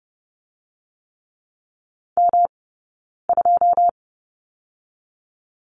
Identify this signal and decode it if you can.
Morse code, heard as a keyed tone, about 30 wpm, 700 Hz. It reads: M 2
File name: Lesson 18d - Letter m - Two Characters 30wpm